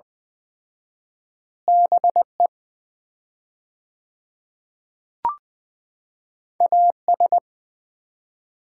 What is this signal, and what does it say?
Morse code, heard as a keyed tone, about 20 wpm, 700 Hz. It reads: BE AS
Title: 4x Word-Spacing - Sets of 3 Words - 20wpm